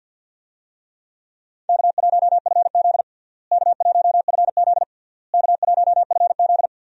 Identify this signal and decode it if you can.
Morse code, heard as a keyed tone, about 48 wpm, 700 Hz. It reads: K1FZ K1FZ K1FZ
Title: USA Call Signs - 4 Characters - Repeated 3x - 50wpm